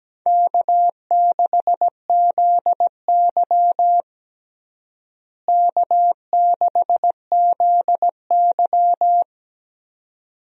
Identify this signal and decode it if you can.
Morse code, heard as a keyed tone, about 17 wpm, 700 Hz. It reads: K6ZY K6ZY
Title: USA Call Signs - 4 Characters - Repeated 2x - 17wpm